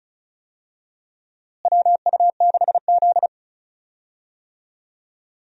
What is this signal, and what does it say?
Morse code, heard as a keyed tone, about 36 wpm, 700 Hz. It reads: WU6Z
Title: USA Call Signs - 4 Characters - Repeated 2x - 35wpm